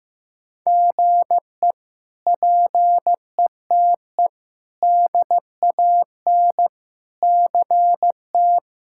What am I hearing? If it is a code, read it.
Morse code, heard as a keyed tone, about 15 wpm, 700 Hz. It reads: GE PETE DAN CT
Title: K1USN SST QSO Practice - 15wpm